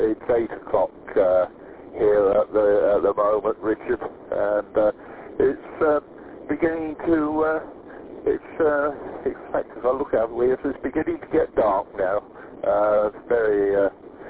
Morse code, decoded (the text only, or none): none